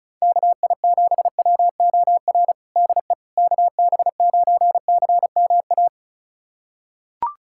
Morse code, heard as a keyed tone, about 36 wpm, 700 Hz. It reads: KI7WOR DE KB9CMA
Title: USA Callsign DE Callsign - 6 Characters - 35wpm